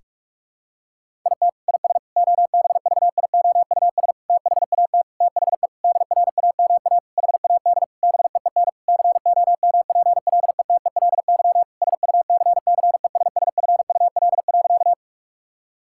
Morse code, heard as a keyed tone, about 46 wpm, 700 Hz. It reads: IT IS OBVIOUS THAT THE DRAMA HAD BEEN COMPLETELY SUCCESSFUL.
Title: Sentences from Top 1600 Words 45wpm